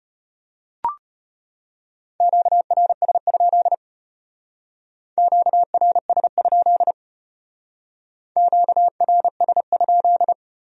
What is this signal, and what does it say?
Morse code, heard as a keyed tone, about 32 wpm, 700 Hz. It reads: QRS? QRS? QRS?